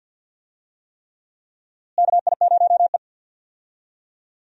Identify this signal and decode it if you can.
Morse code, heard as a keyed tone, about 47 wpm, 700 Hz. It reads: KI0E
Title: USA Call Signs - 4 Characters - Repeated 3x - 50wpm